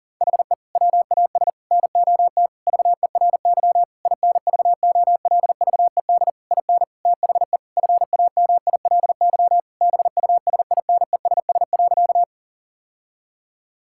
Morse code, heard as a keyed tone, about 37 wpm, 700 Hz. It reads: HE WAS NOT VERY INVOLVED IN THE FAMILY BUSINESS.
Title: Sentences from Top 1000 Words - Review Part 5 - 40wpm